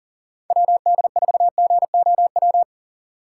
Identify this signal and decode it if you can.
Morse code, heard as a keyed tone, about 40 wpm, 700 Hz. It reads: WD4GOW
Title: USA Call Signs - 6x Character Spacing - 40wpm